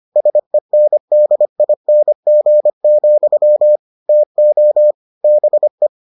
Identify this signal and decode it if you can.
Morse code, heard as a keyed tone, about 25 wpm, 600 Hz. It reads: SENDING, TO BE